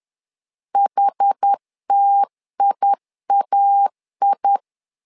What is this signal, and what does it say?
Morse code, heard as a keyed tone, about 10 wpm, 800 Hz. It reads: HTIAI
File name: LEZIONI / 2 UHDN / 2 UH 10wpm 1714